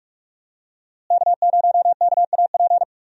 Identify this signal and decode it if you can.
Morse code, heard as a keyed tone, about 46 wpm, 700 Hz. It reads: K0KAP
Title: USA Call Signs - 5 Characters - Repeated 3x - 45wpm